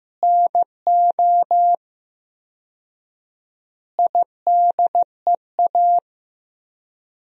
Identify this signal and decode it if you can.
Morse code, heard as a keyed tone, about 15 wpm, 700 Hz. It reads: NO IDEA
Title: 4x Word-Spacing - Sentences from Top 300 Words - 15wpm